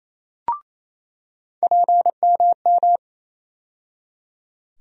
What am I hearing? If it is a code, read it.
Morse code, heard as a keyed tone, about 28 wpm, 700 Hz. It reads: PMM
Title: USA Call Sign Postfixes - 3 Letters 28wpm